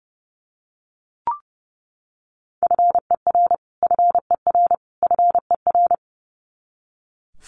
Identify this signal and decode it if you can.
Morse code, heard as a keyed tone, about 30 wpm, 700 Hz. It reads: FER FER FER
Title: Abbreviations Repeated 3x - 30wpm